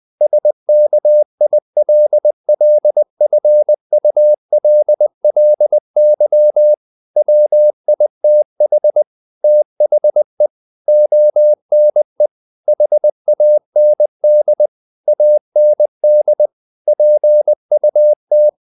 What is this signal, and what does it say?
Morse code, heard as a keyed tone, about 20 wpm, 600 Hz. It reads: SKILLFULLY WITH THE ONE HAND AND PUT
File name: the old man and the sea / the old man and the sea part 1 20wpm